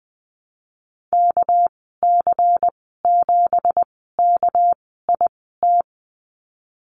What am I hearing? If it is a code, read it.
Morse code, heard as a keyed tone, about 20 wpm, spaced as for 15 wpm, 700 Hz. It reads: KC7KIT
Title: 207 Lesson 33g - Letter j - Call Signs 20wpm